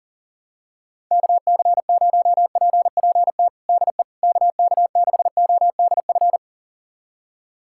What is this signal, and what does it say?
Morse code, heard as a keyed tone, about 40 wpm, 700 Hz. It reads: KC0PPT DE KK6ODF
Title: USA Callsign DE Callsign 40wpm